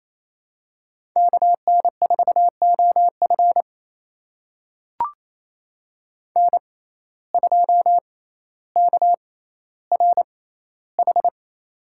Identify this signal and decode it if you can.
Morse code, heard as a keyed tone, about 28 wpm, spaced as for 14 wpm, 700 Hz. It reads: KN4OF N2KRH
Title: USA Call Signs - 5 Characters - 6x Character Spacing - 28wpm